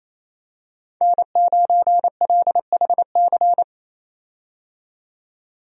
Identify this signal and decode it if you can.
Morse code, heard as a keyed tone, about 28 wpm, 700 Hz. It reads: N9LHC
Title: USA Call Signs - 5 Characters - 28wpm